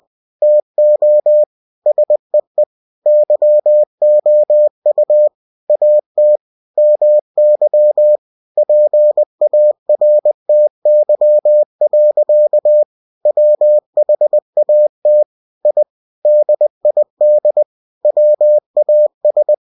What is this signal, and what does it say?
Morse code, heard as a keyed tone, about 20 wpm, 600 Hz. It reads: TO SEE YOU AT MY PARTY. WHAT I DID WAS
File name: cw-books / winnie-the-pooh / winnie-20wpm-page-0107